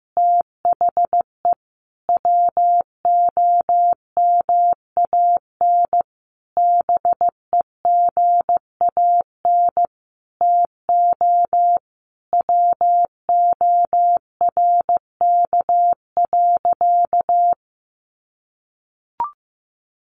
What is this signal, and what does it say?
Morse code, heard as a keyed tone, about 15 wpm, 700 Hz. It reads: THE WOMAN BEGAN TO WORK.
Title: Sentences from Top 1100 Words 15wpm